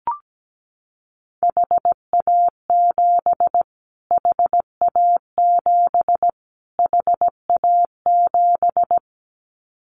Morse code, heard as a keyed tone, about 17 wpm, 700 Hz. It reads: HA7 HA7 HA7
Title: Callsign Prefixes for Top 25 DXCC - Repeated 3x - 17wpm